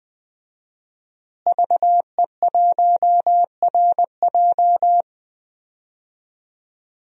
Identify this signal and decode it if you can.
Morse code, heard as a keyed tone, about 21 wpm, 700 Hz. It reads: VE1RJ
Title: DXCC-100 Call Signs - 5 Characters - Repeated 3x - 20wpm